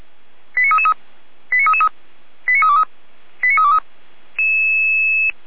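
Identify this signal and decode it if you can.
Five-tone selective call (ZVEI), heard as quick runs of five tones, tones of about 80 ms. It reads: 89333 89333 89323 89323
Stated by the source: Five-tone ZVEI selective call 89330 received by SDR.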